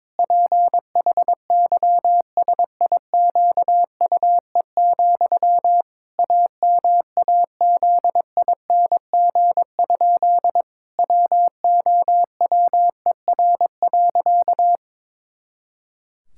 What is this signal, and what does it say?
Morse code, heard as a keyed tone, about 22 wpm, 700 Hz. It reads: PHYSIQUE, AMAZING? WOWER.
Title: Pangram Challenge v2 - 22wpm